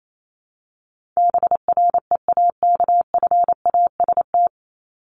Morse code, heard as a keyed tone, about 28 wpm, 700 Hz. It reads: BREAKFAST
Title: Top 1251-1500 Words 28wpm